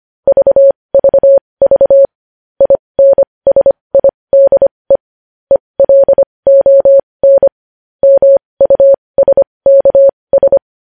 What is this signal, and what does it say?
Morse code, heard as a keyed tone, about 25 wpm, spaced as for 20 wpm, 550 Hz. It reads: VVV INSIDE ELON MUSKS
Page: news-headlines-in-morse-code-at-20-wpm